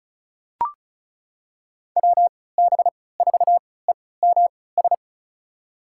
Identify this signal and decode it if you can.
Morse code, heard as a keyed tone, about 35 wpm, spaced as for 20 wpm, 700 Hz. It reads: WB4EMS
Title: USA Call Signs - 6 Characters - 3x Character Spacing - 35wpm